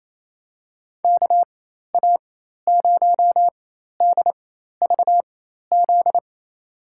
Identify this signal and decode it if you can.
Morse code, heard as a keyed tone, about 28 wpm, spaced as for 13 wpm, 700 Hz. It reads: KA0DVZ
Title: USA Call Signs - 6 Characters - 4x Character Spacing - 28wpm